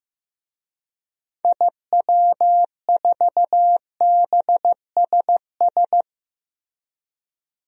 Morse code, heard as a keyed tone, about 16 wpm, 700 Hz. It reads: IW4BSS
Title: DXCC-100 Call Signs - 6 Characters - Repeated 3x - 15wpm